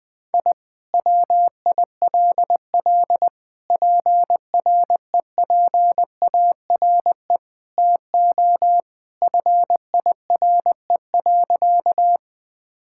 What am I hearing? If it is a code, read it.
Morse code, heard as a keyed tone, about 20 wpm, 700 Hz. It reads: I WILL PREPARE TO FIRE.
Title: Sentences from Top 1000 Words - Review Part 6 - 20wpm